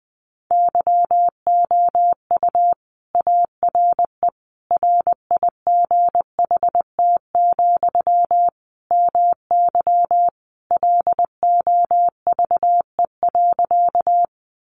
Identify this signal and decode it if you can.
Morse code, heard as a keyed tone, about 20 wpm, 700 Hz. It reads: YOU ARE RIGHT, MY LOVE.
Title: Sentences from Top 400 Words 20wpm